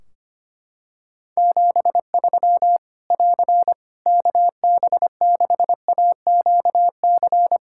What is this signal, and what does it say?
Morse code, heard as a keyed tone, about 25 wpm, 700 Hz. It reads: 73 <AR> KB6AQC
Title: Practice QSO - Exchange 4 - Ending - 25wpm